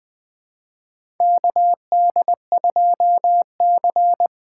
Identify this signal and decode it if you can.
Morse code, heard as a keyed tone, about 21 wpm, 700 Hz. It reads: KD2C
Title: USA Call Signs - 4 Characters - 4x Character Spacing - 20wpm